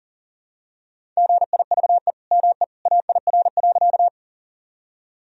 Morse code, heard as a keyed tone, about 40 wpm, 700 Hz. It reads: GIVE ME AIR.
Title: Sentences from Top 1000 Words - Review Part 1 - 40wpm